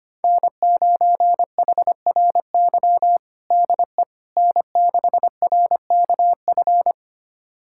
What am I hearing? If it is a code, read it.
Morse code, heard as a keyed tone, about 25 wpm, 700 Hz. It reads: N9HRY DE N6RKF
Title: USA Callsign DE Callsign - 5 Characters - 25wpm